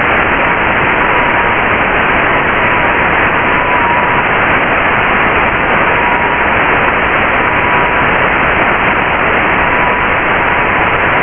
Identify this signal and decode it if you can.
FT8.
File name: Radio / SDR Audio / audio z 19-48-00 11-09-2022